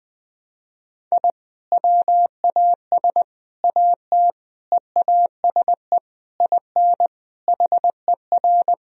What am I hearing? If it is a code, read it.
Morse code, heard as a keyed tone, about 20 wpm, 700 Hz. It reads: I WAS AT EASE IN HER
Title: Sentences from Top 1000 Words - Review Part 1 - 20wpm